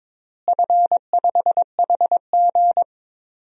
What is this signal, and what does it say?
Morse code, heard as a keyed tone, about 21 wpm, 700 Hz. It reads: F5HG
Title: DXCC-100 Call Signs - 4 Characters - Repeated 3x - 22wpm